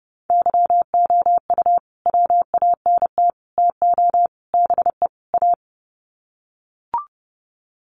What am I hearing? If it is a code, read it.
Morse code, heard as a keyed tone, about 30 wpm, 700 Hz. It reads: YOU WANT TO BE A
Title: Sets of 5 Words Encore2 30wpm